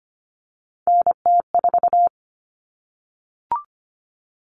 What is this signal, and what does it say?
Morse code, heard as a keyed tone, about 25 wpm, 700 Hz. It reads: NT4
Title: All Cut Numbers - 3-Digit 25wpm